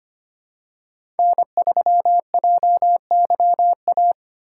Morse code, heard as a keyed tone, about 25 wpm, 700 Hz. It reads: N3JYA
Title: USA Call Signs - 5 Characters - 3x Character Spacing - 25wpm